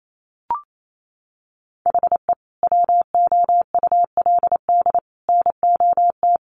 Morse code, heard as a keyed tone, about 28 wpm, 700 Hz. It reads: HE WOULD NOT
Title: Sentences from Top 1300 Words 28wpm